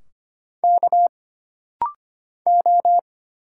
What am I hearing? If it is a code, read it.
Morse code, heard as a keyed tone, about 25 wpm, 700 Hz. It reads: K O
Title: Single Letters - Rapid-Fire 25wpm